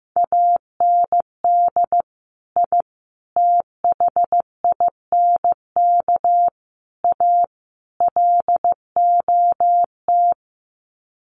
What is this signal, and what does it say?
Morse code, heard as a keyed tone, about 15 wpm, 700 Hz. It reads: AND I THINK A LOT
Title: Sets of 5 Words Encore2 15wpm